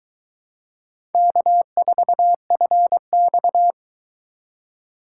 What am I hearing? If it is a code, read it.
Morse code, heard as a keyed tone, about 24 wpm, 700 Hz. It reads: K4FX